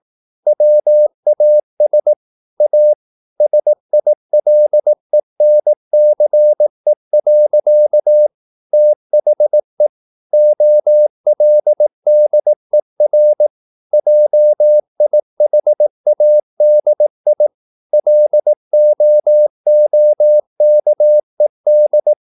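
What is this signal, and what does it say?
Morse code, heard as a keyed tone, about 18 wpm, 600 Hz. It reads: WAS A SILENCE. THE OLDER JIHADI LOOKED
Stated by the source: Morse practice at 18 wpm, English text, from CW med Gnister.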